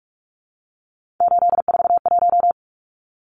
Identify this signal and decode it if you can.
Morse code, heard as a keyed tone, about 44 wpm, 700 Hz. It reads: 841
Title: Numbers - 3-Digit 45wpm